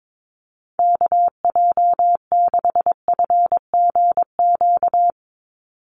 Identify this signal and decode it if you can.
Morse code, heard as a keyed tone, about 22 wpm, 700 Hz. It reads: KJ6FGQ
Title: USA Call Signs 22wpm